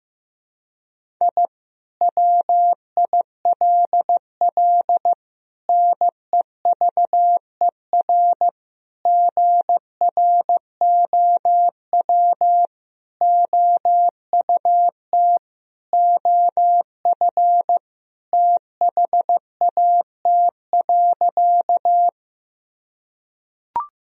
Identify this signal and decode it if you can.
Morse code, heard as a keyed tone, about 15 wpm, 700 Hz. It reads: I WILL NEVER GROW OUT OF THAT.
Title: Sentences from Top 1000 Words - Review Part 1 - 15wpm